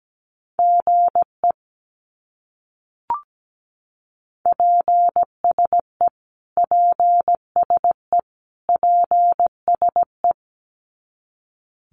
Morse code, heard as a keyed tone, about 17 wpm, 700 Hz. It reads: GE PSE PSE PSE
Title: Abbreviations Repeated 3x - 17wpm